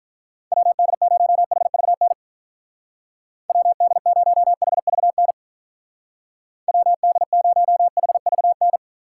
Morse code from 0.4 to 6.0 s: WD0HVN WD0HVN

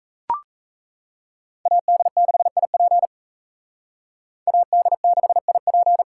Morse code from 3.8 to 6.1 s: AD6IP

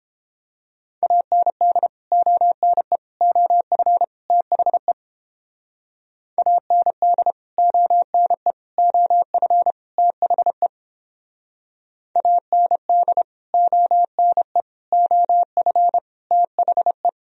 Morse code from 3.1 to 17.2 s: OF THE AND ONE OF THE AND ONE OF THE

